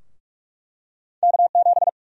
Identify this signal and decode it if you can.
Morse code, heard as a keyed tone, about 47 wpm, 700 Hz. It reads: K7
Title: USA Call Sign Prefixes - Repeated 2x - 45wpm